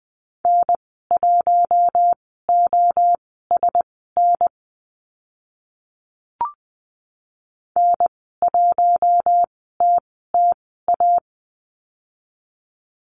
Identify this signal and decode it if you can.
Morse code, heard as a keyed tone, about 20 wpm, spaced as for 14 wpm, 700 Hz. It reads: N1OSN N1TTA
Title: 042 Lesson 8g - Number 1 - Call Signs 20wpm